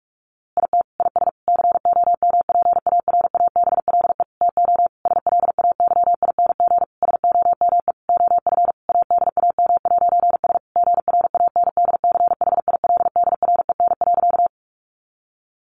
Morse code, heard as a keyed tone, about 44 wpm, 700 Hz. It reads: IT IS COMPARABLE TO SLAYING SOME OF ADAM'S GRANDCHILDREN.
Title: A Sound Of Thunder 45wpm